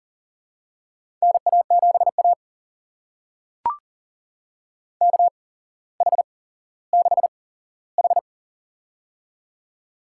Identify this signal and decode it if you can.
Morse code, heard as a keyed tone, about 40 wpm, spaced as for 17 wpm, 700 Hz. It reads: NA7A KH6H
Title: USA Call Signs - 4 Characters - 8x Character Spacing - 40wpm